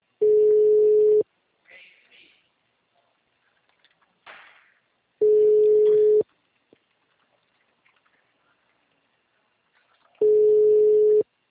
Ringback tone (Europe, 425 Hz).